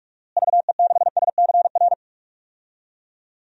Morse code, heard as a keyed tone, about 46 wpm, 700 Hz. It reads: VE6SCR
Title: DXCC-100 Call Signs - 6 Characters - Repeated 3x - 45wpm